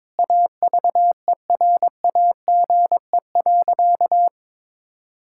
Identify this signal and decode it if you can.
Morse code, heard as a keyed tone, about 22 wpm, 700 Hz. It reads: AVERAGE.
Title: Sentences from Top 1000 Words - Review Part 6 - 22wpm